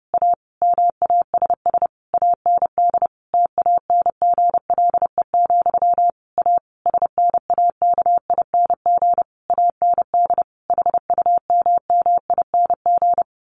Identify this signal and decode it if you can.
Morse code, heard as a keyed tone, about 30 wpm, 700 Hz. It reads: A MASS AND TANGLE, A SNAKING AND HUMMING
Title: A Sound Of Thunder 30wpm